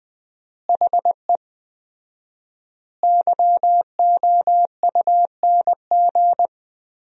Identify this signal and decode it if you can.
Morse code, heard as a keyed tone, about 20 wpm, 700 Hz. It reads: HE YOUNG
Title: 4x Word-Spacing - Sentences from Top 400 Words - 20wpm